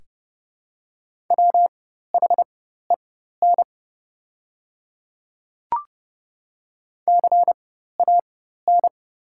Morse code, heard as a keyed tone, about 30 wpm, spaced as for 14 wpm, 700 Hz. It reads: WHEN CAN
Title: 4x Character Spacing - Top 100 Words - 30wpm